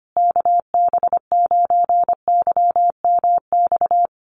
Morse code, heard as a keyed tone, about 25 wpm, 700 Hz. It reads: KB9YMX
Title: USA Call Signs Repeated 3x 25wpm